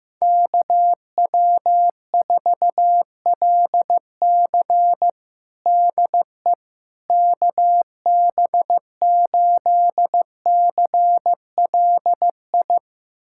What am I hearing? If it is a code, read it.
Morse code, heard as a keyed tone, about 15 wpm, 700 Hz. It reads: KW4LC DE KB8CLI
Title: USA Callsign DE Callsign 15wpm